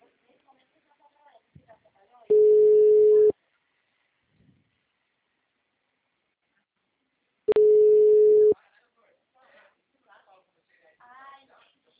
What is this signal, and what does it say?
Ringback tone (Europe, 425 Hz), with speech.